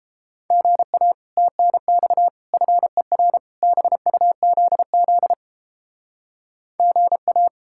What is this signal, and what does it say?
Morse code, heard as a keyed tone, about 29 wpm, 700 Hz. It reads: GA TNX FER BUZZ GA